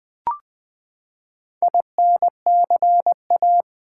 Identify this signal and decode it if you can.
Morse code, heard as a keyed tone, about 20 wpm, 700 Hz. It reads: INCA
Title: Build A Word - 4 Letters At A Time - 11-Letter Words - 20wpm